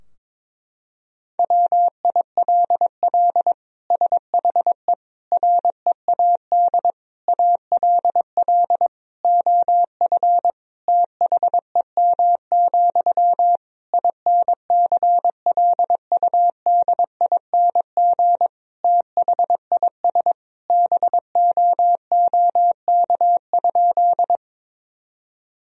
Morse code, heard as a keyed tone, about 22 wpm, 700 Hz. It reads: WILL SHE READ ALL OF THEM, INCLUDING THIS BOOK?
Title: Sentences from Top 1000 Words - Review Part 4 - 22wpm